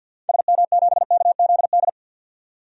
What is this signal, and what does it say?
Morse code, heard as a keyed tone, about 53 wpm, 700 Hz. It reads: SM7KZD
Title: DXCC-100 Call Signs - 6 Characters - 50wpm